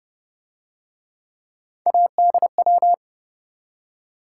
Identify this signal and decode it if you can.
Morse code, heard as a keyed tone, about 28 wpm, 700 Hz. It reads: ADW